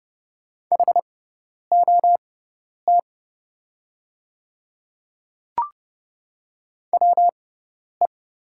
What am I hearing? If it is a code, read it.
Morse code, heard as a keyed tone, about 31 wpm, spaced as for 10 wpm, 700 Hz. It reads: HOT WE
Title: 6x Character Spacing - Top 100 Words - 30wpm